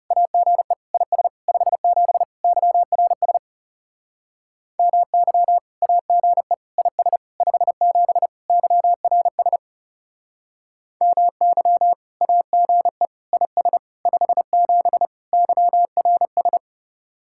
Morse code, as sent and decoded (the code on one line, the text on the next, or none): .- --. / .. ... / ..... --... / -.-- .-. ... / -- -.-- / .- --. / .. ... / ..... --... / -.-- .-. ... / -- -.-- / .- --. / .. ... / ..... --... / -.-- .-. ...
AGE IS 57 YRS MY AGE IS 57 YRS MY AGE IS 57 YRS